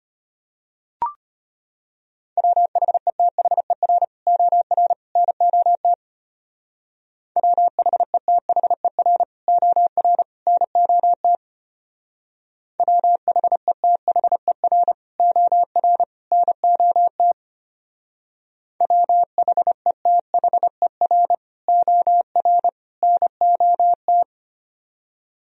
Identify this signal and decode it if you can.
Morse code, heard as a keyed tone, about 30 wpm, 700 Hz. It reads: WHETHER OR NOT WHETHER OR NOT WHETHER OR NOT WHETHER OR NOT
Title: Speed-Racing - Sets of 3 Words 25wpm